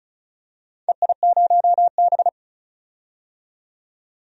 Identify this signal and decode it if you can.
Morse code, heard as a keyed tone, about 36 wpm, 700 Hz. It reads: EI0B
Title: DXCC-100 Call Signs - 4 Characters - 35wpm